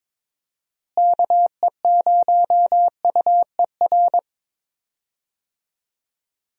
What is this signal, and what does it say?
Morse code, heard as a keyed tone, about 22 wpm, 700 Hz. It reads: KE0UER
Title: USA Call Signs - 6 Characters - Repeated 3x - 22wpm